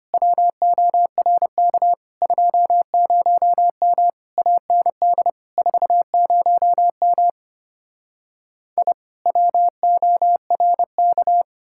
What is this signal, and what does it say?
Morse code, heard as a keyed tone, about 33 wpm, 700 Hz. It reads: WORK 20M AND 40M I WORK